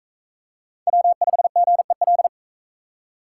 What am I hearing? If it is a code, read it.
Morse code, heard as a keyed tone, about 36 wpm, 700 Hz. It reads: W5GEL